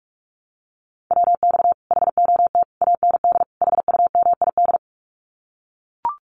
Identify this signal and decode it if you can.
Morse code, heard as a keyed tone, about 44 wpm, 700 Hz. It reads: WX HOT AND HUMID
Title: Common QSO Phrases Encore 45wpm